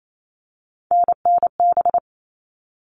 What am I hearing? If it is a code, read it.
Morse code, heard as a keyed tone, about 28 wpm, 700 Hz. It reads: NNB
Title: All Cut Numbers - 3-Digit 28wpm